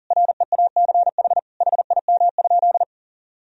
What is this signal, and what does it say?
Morse code, heard as a keyed tone, about 40 wpm, 700 Hz. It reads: REACH HIM?
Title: Sentences from Top 1000 Words - Review Part 2 - 40wpm